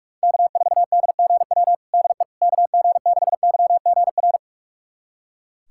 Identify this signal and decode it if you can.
Morse code, heard as a keyed tone, about 42 wpm, 700 Hz. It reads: K4DGW DE KG6YGR
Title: USA Callsign DE Callsign 45wpm